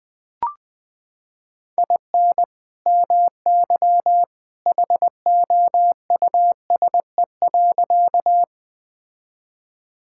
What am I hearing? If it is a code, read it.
Morse code, heard as a keyed tone, about 20 wpm, 700 Hz. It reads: IN MY HOUSE.
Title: Sentences with 2-Word Phrases - 20wpm